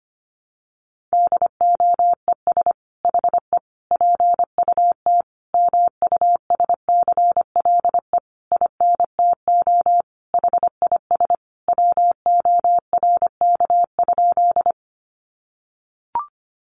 Morse code, heard as a keyed tone, about 25 wpm, 700 Hz. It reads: DOES HE PUT MUSCLE INTO HIS WORK?
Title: Sentences from Top 1300 Words 25wpm